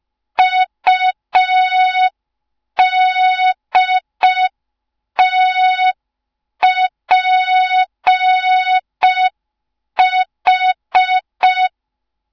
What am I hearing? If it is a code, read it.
Morse code, heard as a keyed tone, about 5 wpm, 3700 Hz. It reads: UDTPH
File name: Morze-HA2MN / hang / 07 GYAKORLAT